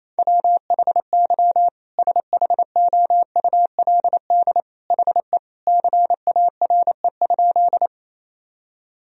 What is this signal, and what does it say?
Morse code, heard as a keyed tone, about 28 wpm, 700 Hz. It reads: WHY SHOULD HE CARE?
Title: Sentences from Top 1000 Words - Review Part 1 - 28wpm